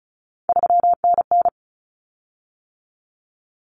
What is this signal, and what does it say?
Morse code, heard as a keyed tone, about 34 wpm, 700 Hz. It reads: 3NN
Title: Common Cut Numbers - 3-Digit 35wpm